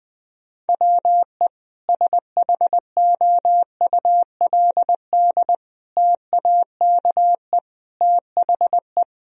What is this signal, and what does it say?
Morse code, heard as a keyed tone, about 21 wpm, 700 Hz. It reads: WE SHOULD TAKE THE
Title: Sentences from Top 1000 Words - Review Part 2 - 20wpm